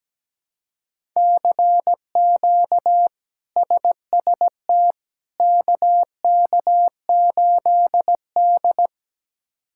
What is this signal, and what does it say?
Morse code, heard as a keyed tone, about 17 wpm, 700 Hz. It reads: CQ SST KK8D
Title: K1USN SST QSO Practice - 17wpm